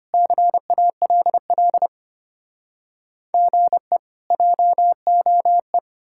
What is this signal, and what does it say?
Morse code, heard as a keyed tone, about 33 wpm, 700 Hz. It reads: CALL GE JOE